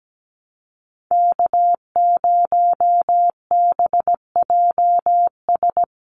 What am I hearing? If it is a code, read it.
Morse code, heard as a keyed tone, about 17 wpm, 700 Hz. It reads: K0BJS
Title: USA Call Signs 17wpm